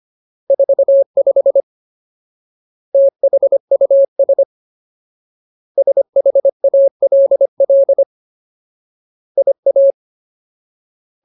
Morse code, heard as a keyed tone, about 25 wpm, 550 Hz. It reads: 45 THUS SHALL IA
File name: Ultimate Word List / 25WPM Ultimate